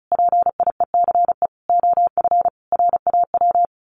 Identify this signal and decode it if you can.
Morse code, heard as a keyed tone, about 34 wpm, 700 Hz. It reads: PIECE OF RAW
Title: Sentences from Top 1500 Words 35wpm